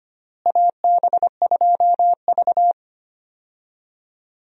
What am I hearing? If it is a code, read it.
Morse code, heard as a keyed tone, about 25 wpm, 700 Hz. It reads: AB2V
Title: USA Call Signs - 4 Characters - Repeated 2x - 25wpm